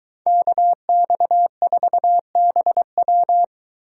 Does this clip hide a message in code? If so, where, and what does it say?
Yes: Morse code from 0.3 to 3.4 s, heard as a keyed tone, about 22 wpm, 700 Hz. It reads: KX4BW